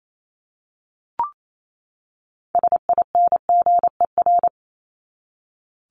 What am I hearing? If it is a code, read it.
Morse code, heard as a keyed tone, about 28 wpm, 700 Hz. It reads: SINGER
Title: Top 1501-1750 Words 28wpm